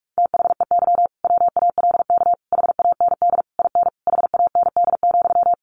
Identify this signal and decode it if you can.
Morse code, heard as a keyed tone, about 44 wpm, 700 Hz. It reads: THEY WALK HAND IN HAND,
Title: Sentences from Top 1400 Words 45wpm